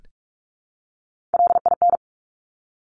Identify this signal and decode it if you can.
Morse code, heard as a keyed tone, about 46 wpm, 700 Hz. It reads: LIN
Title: Top Three Letter Combinations in Words 45wpm